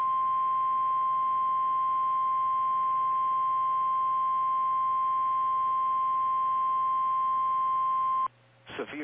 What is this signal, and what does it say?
Emergency Alert System (SAME) tones (the 1050 Hz attention tone), with speech. It is heard from 0.5 to 8.0 s.
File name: eas / KKPRKXPN / KKPRKXPN